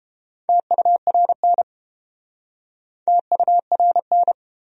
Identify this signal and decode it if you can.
Morse code, heard as a keyed tone, about 31 wpm, 700 Hz. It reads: TURN TURN